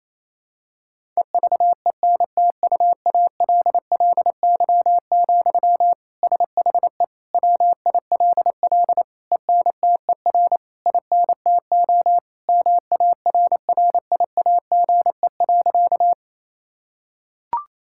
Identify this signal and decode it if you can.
Morse code, heard as a keyed tone, about 28 wpm, 700 Hz. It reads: EVENTUALLY, SHE WILL ENTER INTO MARRIAGE.
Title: Sentences from Top 1000 Words - Review Part 6 - 28wpm